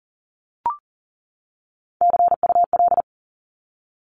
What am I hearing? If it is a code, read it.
Morse code, heard as a keyed tone, about 40 wpm, 700 Hz. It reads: CUL
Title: Abbreviations 40wpm